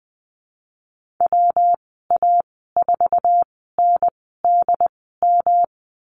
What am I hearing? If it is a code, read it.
Morse code, heard as a keyed tone, about 19 wpm, spaced as for 14 wpm, 700 Hz. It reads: WA4NDM
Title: 147 Lesson 24g - Letter y - Call Signs 20wpm